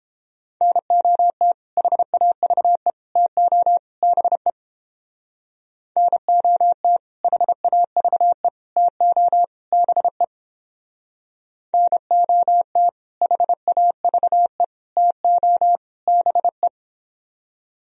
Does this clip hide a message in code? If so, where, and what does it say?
Yes: Morse code from 0.6 to 16.7 s, heard as a keyed tone, about 31 wpm, 700 Hz. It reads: NOT HAVE TO BE NOT HAVE TO BE NOT HAVE TO BE